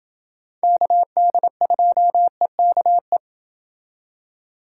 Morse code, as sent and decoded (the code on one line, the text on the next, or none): -.- -.. ..--- . -.- .
KD2EKE